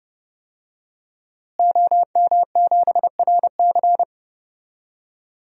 Morse code, heard as a keyed tone, about 30 wpm, 700 Hz. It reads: OM7RC
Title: DXCC-100 Call Signs - 5 Characters - Repeated 2x - 30wpm